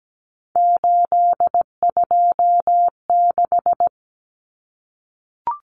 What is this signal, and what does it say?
Morse code, heard as a keyed tone, about 17 wpm, 700 Hz. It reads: 826
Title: Common Cut Numbers - 3-Digit 17wpm